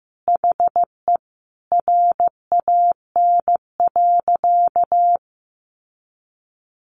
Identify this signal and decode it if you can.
Morse code, heard as a keyed tone, about 15 wpm, 700 Hz. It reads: HE RAN.
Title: Sentences from Top 600 Words 15wpm